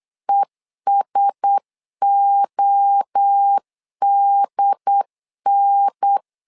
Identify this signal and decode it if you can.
Morse code, heard as a keyed tone, about 8 wpm, 800 Hz. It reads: ESODN
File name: LEZIONI / 4 Q1S 8wpm 2710